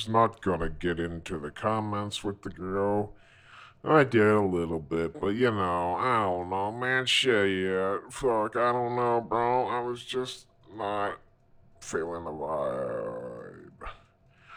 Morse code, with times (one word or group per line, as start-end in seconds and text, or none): none